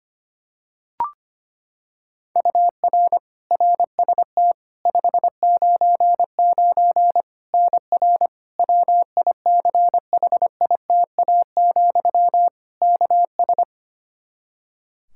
Morse code, heard as a keyed tone, about 25 wpm, 700 Hz. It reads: UR RST 599 NR WICHITA, KS
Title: K1USN SST QSO Practice 25wpm